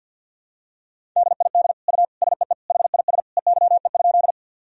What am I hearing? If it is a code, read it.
Morse code, heard as a keyed tone, about 47 wpm, 700 Hz. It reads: DID U SEE HIS EYE?
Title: Sentences from Top 300 Words and Abbreviations 50wpm